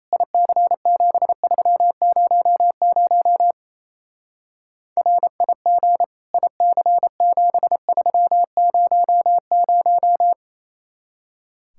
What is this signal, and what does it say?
Morse code, heard as a keyed tone, about 37 wpm, 700 Hz. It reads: IC7300 RIG IC7300